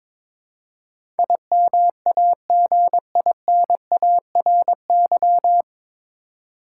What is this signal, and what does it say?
Morse code, heard as a keyed tone, about 21 wpm, 700 Hz. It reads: IMAGINARY
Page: Morse Code Ninja practice 22wpm